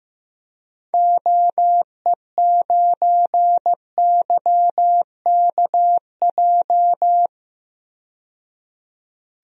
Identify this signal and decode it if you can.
Morse code, heard as a keyed tone, about 16 wpm, 700 Hz. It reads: OE9YKJ
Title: DXCC-100 Call Signs - 6 Characters - 15wpm